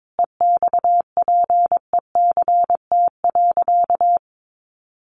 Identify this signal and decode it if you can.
Morse code, heard as a keyed tone, about 22 wpm, 700 Hz. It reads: EXPECT.
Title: Sentences from Top 1100 Words 22wpm